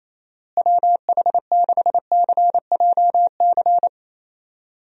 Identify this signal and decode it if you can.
Morse code, heard as a keyed tone, about 28 wpm, 700 Hz. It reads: WH6CJC
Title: USA Call Signs - 6 Characters - Repeated 3x - 28wpm